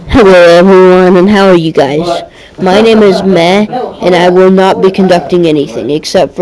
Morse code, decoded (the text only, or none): none